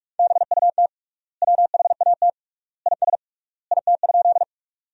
Morse code, heard as a keyed tone, about 42 wpm, 700 Hz. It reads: BUT WHAT IS IT?
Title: 3x Word-Spacing - Sentences from Top 100 Words - 45wpm